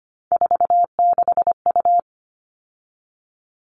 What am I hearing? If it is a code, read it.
Morse code, heard as a keyed tone, about 25 wpm, 700 Hz. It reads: 46U
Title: All Cut Numbers - 3-Digit 25wpm